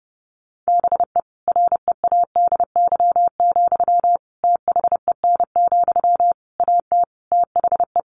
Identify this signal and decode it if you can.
Morse code, heard as a keyed tone, about 30 wpm, 700 Hz. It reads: BE READY, THEN, AT THE
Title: Sentences from Top 1400 Words 30wpm